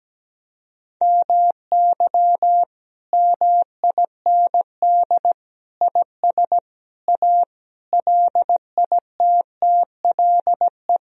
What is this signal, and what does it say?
Morse code, heard as a keyed tone, about 17 wpm, 700 Hz. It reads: MY MIND IS A LITTLE
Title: Sentences from Top 1000 Words - Review Part 1 - 17wpm